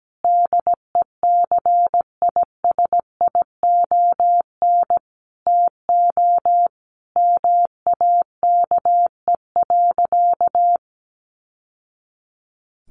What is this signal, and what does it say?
Morse code, heard as a keyed tone, about 17 wpm, 700 Hz. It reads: DECISION TO MAKE.
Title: Sentences from Top 1000 Words 17wpm